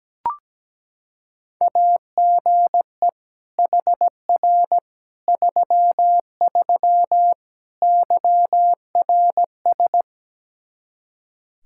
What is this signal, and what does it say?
Morse code, heard as a keyed tone, about 17 wpm, 700 Hz. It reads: AGE HR 33 YRS
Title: Practice QSO - Exchange 3 - More Info - 17wpm